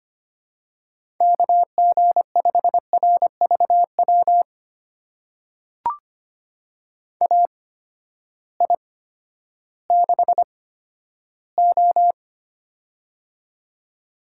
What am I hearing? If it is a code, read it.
Morse code, heard as a keyed tone, about 25 wpm, spaced as for 12 wpm, 700 Hz. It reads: KG5RVW AI6O
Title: USA Call Signs - 8x Character Spacing - 25wpm